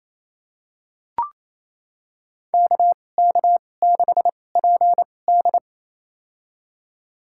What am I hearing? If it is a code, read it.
Morse code, heard as a keyed tone, about 28 wpm, spaced as for 20 wpm, 700 Hz. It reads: KK6PD
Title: USA Call Signs - 2x Character Spacing - 28wpm